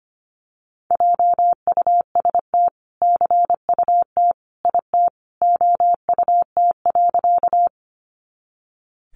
Morse code, heard as a keyed tone, about 25 wpm, 700 Hz. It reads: JUST CUT IT OUT.
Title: Sentences from Top 300 Words 25wpm